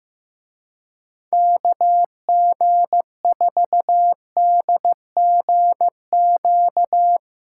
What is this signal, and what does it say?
Morse code, heard as a keyed tone, about 16 wpm, 700 Hz. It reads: KG4DGQ
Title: Speed Racing - USA Call Signs 15wpm